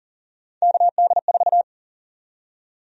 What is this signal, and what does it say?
Morse code, heard as a keyed tone, about 40 wpm, 700 Hz. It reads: KD4
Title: USA Call Sign Prefixes - 40wpm